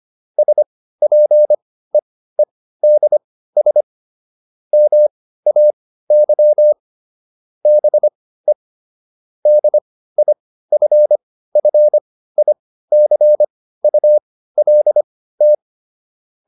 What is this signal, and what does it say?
Morse code, heard as a keyed tone, about 25 wpm, spaced as for 15 wpm, 600 Hz. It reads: SPEEDS MAY BE DIFFICULT